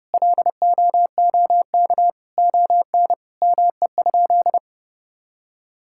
Morse code, heard as a keyed tone, about 30 wpm, 700 Hz. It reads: LOOK ON ME?
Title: Sentences from Top 1000 Words - Review Part 5 - 30wpm